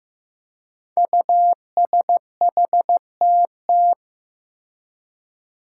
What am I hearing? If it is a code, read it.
Morse code, heard as a keyed tone, about 15 wpm, 700 Hz. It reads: USHTT
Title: Five Challenging Letter-Number Combinations 15wpm